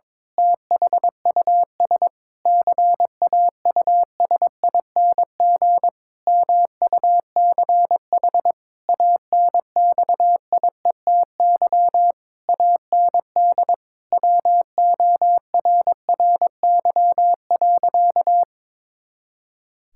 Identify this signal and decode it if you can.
Morse code, heard as a keyed tone, about 22 wpm, 700 Hz. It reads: THUS CAUSING MUCH ANXIETY AND WORRY.
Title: Pangram Challenge v2 - 22wpm